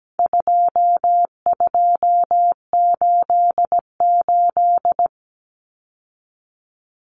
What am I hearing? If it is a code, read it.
Morse code, heard as a keyed tone, about 17 wpm, 700 Hz. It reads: 2288
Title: Common Cut Numbers - 4-Digit 17wpm